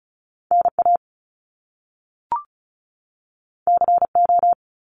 Morse code, heard as a keyed tone, about 36 wpm, 700 Hz. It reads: NA CO
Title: Top Two Letter Combinations in Words 35wpm